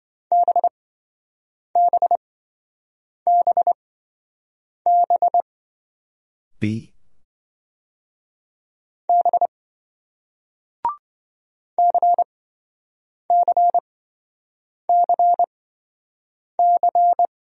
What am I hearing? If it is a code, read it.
Morse code, heard as a keyed tone, about 27 wpm, spaced as for 7 wpm, 700 Hz. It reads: BBBB B CCCC